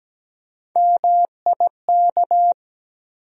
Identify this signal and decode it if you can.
Morse code, heard as a keyed tone, about 17 wpm, 700 Hz. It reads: MIK
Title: USA Call Sign Postfixes - 3 Letters 17wpm